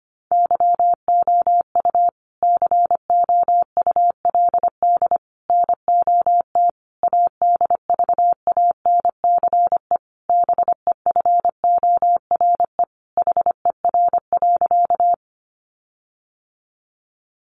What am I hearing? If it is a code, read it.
Morse code, heard as a keyed tone, about 25 wpm, 700 Hz. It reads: YOU COULD NOT ADVANCE BEFORE HER.
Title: Sentences from Top 1400 Words 25wpm